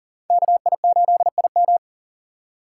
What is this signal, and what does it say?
Morse code, heard as a keyed tone, about 40 wpm, 700 Hz. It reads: KI8IM
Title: USA Call Signs - 3x Character Spacing - 40wpm